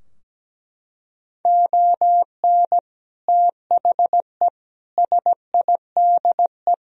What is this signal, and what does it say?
Morse code, heard as a keyed tone, about 17 wpm, 700 Hz. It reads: ON THE SIDE
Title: Sentences from Top 1000 Words - Review Part 3 - 17wpm